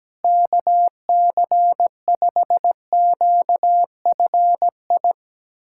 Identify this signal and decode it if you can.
Morse code, heard as a keyed tone, about 17 wpm, 700 Hz. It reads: KC5QFI
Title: USA Callsign DE Callsign - 6 Characters - 17wpm